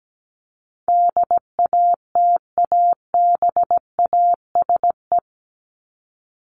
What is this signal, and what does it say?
Morse code, heard as a keyed tone, about 17 wpm, 700 Hz. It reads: DATABASE
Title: Top 1751-2000 Words 17wpm